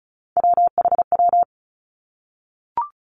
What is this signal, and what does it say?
Morse code, heard as a keyed tone, about 36 wpm, 700 Hz. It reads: WHW
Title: Three Challenging Letter-Number Combinations 35wpm